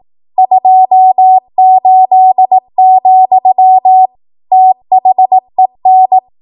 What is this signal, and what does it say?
Morse code, heard as a keyed tone, about 18 wpm, 750 Hz. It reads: 28, THEN
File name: Archive / 18 WPM / 18WPM